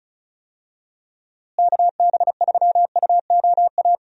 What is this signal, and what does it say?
Morse code, heard as a keyed tone, about 36 wpm, 700 Hz. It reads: KB3UOA
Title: USA Call Signs - 6 Characters - Repeated 2x - 35wpm